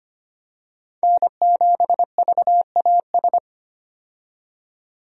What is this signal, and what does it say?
Morse code, heard as a keyed tone, about 25 wpm, 700 Hz. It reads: N7VAS
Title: USA Call Signs - 5 Characters - 25wpm